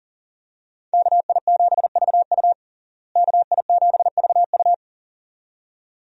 Morse code, heard as a keyed tone, about 40 wpm, 700 Hz. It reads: KI7VU KI7VU
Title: USA Call Signs - 5 Characters - Repeated 2x - 40wpm